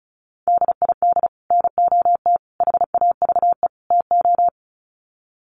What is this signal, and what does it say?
Morse code, heard as a keyed tone, about 34 wpm, 700 Hz. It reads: DID NOT HAVE TO
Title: Sets of 4 Words 35wpm